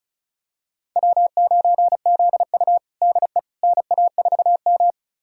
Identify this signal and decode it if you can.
Morse code, heard as a keyed tone, about 36 wpm, 700 Hz. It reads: W9ZU DE NA4M
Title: USA Callsign DE Callsign - 4 Characters - 35wpm